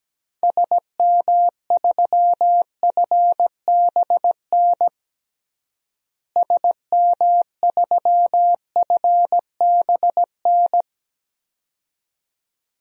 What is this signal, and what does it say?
Morse code, heard as a keyed tone, about 17 wpm, 700 Hz. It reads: SM3FBN SM3FBN
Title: DXCC-100 Call Signs - 6 Characters - Repeated 2x - 17wpm